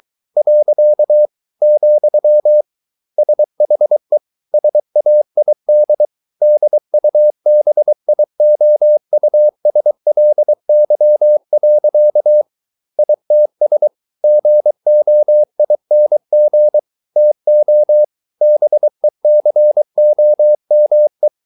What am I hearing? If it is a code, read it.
Morse code, heard as a keyed tone, about 23 wpm, 600 Hz. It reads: . , SHE SAID DUBIOUSLY. ITS GOING TO BECOME